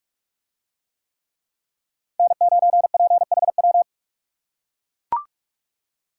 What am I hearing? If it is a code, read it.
Morse code, heard as a keyed tone, about 42 wpm, 700 Hz. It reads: N9PHW